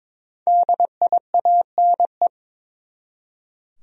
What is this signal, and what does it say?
Morse code, heard as a keyed tone, about 22 wpm, 700 Hz. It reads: DIANE
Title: USA Common Names - Max 5 Letters - 22wpm